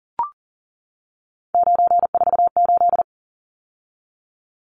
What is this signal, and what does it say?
Morse code, heard as a keyed tone, about 40 wpm, 700 Hz. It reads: 948
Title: Numbers - 3-Digit 40wpm